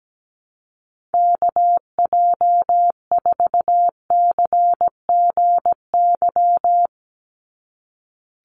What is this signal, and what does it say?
Morse code, heard as a keyed tone, about 17 wpm, 700 Hz. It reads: KJ4CGY
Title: USA Call Signs 17wpm